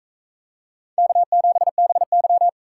Morse code, heard as a keyed tone, about 42 wpm, 700 Hz. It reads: K7BY